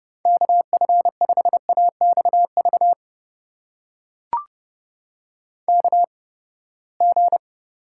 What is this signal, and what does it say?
Morse code, heard as a keyed tone, about 32 wpm, spaced as for 21 wpm, 700 Hz. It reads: KF5AXV KG
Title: USA Call Signs - 6 Characters - 8x Character Spacing - 30wpm